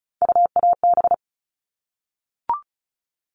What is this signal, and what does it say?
Morse code, heard as a keyed tone, about 34 wpm, 700 Hz. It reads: UAB
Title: All Cut Numbers - 3-Digit 35wpm